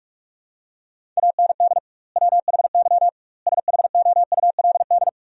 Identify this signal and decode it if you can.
Morse code, heard as a keyed tone, about 41 wpm, 700 Hz. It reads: AND WHY SHOULD
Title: 2x Word-Spacing - Sentences from Top 300 Words - 45wpm